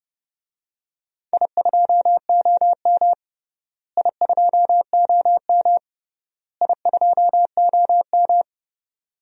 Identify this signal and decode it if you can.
Morse code, heard as a keyed tone, about 30 wpm, 700 Hz. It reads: I2OM I2OM I2OM
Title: DXCC-100 Call Signs - 4 Characters - Repeated 3x - 30wpm